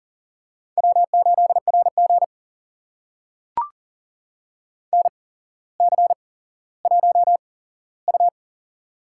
Morse code, heard as a keyed tone, about 40 wpm, spaced as for 17 wpm, 700 Hz. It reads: W8RG NC1U
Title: USA Call Signs - 4 Characters - 8x Character Spacing - 40wpm